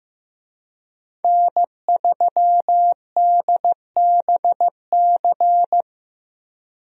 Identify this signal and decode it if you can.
Morse code, heard as a keyed tone, about 15 wpm, 700 Hz. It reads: N3DBC
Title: USA Call Signs - 5 Characters - Repeated 2x - 15wpm